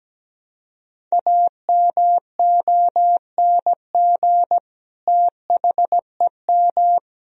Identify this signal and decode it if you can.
Morse code, heard as a keyed tone, about 17 wpm, 700 Hz. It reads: AMONG THEM
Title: Sentences from Top 1000 Words - Review Part 3 - 17wpm